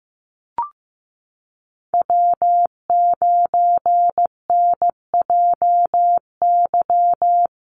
Morse code, heard as a keyed tone, about 15 wpm, 700 Hz. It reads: W9NJY
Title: Call Signs Repeated 3x - 15wpm